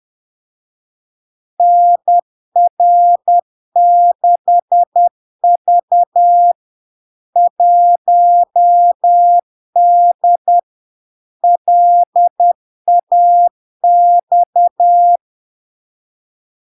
Morse code, heard as a keyed tone, about 10 wpm, 700 Hz. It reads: NR6V 1D LAX
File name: LARK-FD-2022-Log-in-CW-10wpm-700-Hz